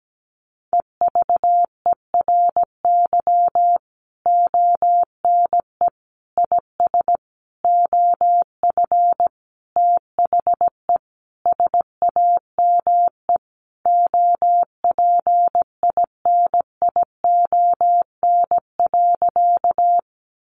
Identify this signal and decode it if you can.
Morse code, heard as a keyed tone, about 17 wpm, 700 Hz. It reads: EVERY ONE IS OF THE SAME OPINION.
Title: Sentences from Top 1200 Words 17wpm